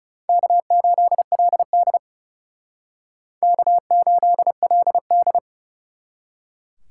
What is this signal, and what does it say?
Morse code, heard as a keyed tone, about 37 wpm, 700 Hz. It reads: K8LD K8LD
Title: Speed Racing - USA 1x2 Call Signs 30wpm